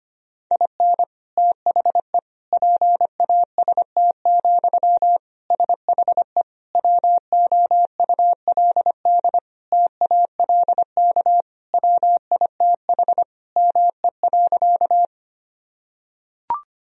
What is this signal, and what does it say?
Morse code, heard as a keyed tone, about 25 wpm, 700 Hz. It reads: IN THE PAST, SHE WOULD TALK WITH ME.
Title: Sentences from Top 1000 Words - Review Part 3 - 25wpm